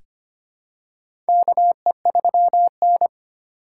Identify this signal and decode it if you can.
Morse code, heard as a keyed tone, about 25 wpm, 700 Hz. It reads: KE3N
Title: USA Call Signs - 3x Character Spacing - 25wpm